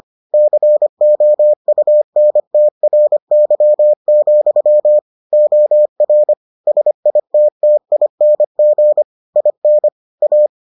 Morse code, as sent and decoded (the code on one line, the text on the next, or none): -.-. --- ..- -. - .-. -.-- --..-- / --- .-. / ... .. - - .. -. --. / .. -. / .-
COUNTRY, OR SITTING IN A